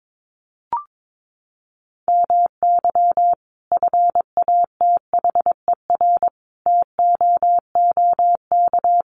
Morse code, heard as a keyed tone, about 22 wpm, 700 Hz. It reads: MY FATHER TOOK
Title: Sentences from Top 1500 Words 22wpm